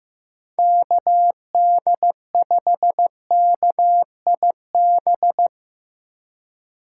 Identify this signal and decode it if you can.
Morse code, heard as a keyed tone, about 15 wpm, 700 Hz. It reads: KD5KIB
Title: USA Call Signs - 6 Characters - 4x Character Spacing - 15wpm